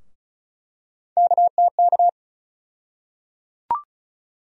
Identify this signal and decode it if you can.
Morse code, heard as a keyed tone, about 36 wpm, 700 Hz. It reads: KTK
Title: USA Call Sign Postfixes - 3 Letters 35wpm